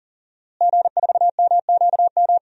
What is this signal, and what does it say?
Morse code, heard as a keyed tone, about 40 wpm, 700 Hz. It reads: G4MQM
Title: DXCC-100 Call Signs - 5 Characters - Repeated 3x - 40wpm